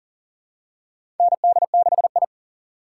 Morse code, heard as a keyed tone, about 40 wpm, 700 Hz. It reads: ND6I
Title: USA Call Signs - 4 Characters - 4x Character Spacing - 40wpm